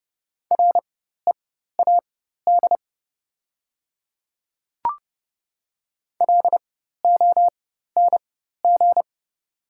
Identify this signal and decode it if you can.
Morse code, heard as a keyed tone, about 29 wpm, spaced as for 14 wpm, 700 Hz. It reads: READ LONG
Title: 4x Character Spacing - Top 200 Words - 30wpm